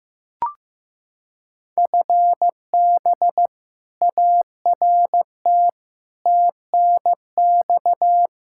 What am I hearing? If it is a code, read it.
Morse code, heard as a keyed tone, about 15 wpm, 700 Hz. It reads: FB ART TNX
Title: K1USN SST QSO Practice 15wpm